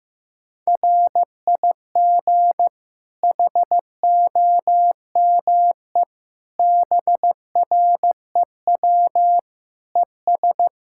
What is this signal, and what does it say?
Morse code, heard as a keyed tone, about 15 wpm, 700 Hz. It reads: RIG HOME BREW ES
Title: Practice QSO - Exchange 2 - Info - 15wpm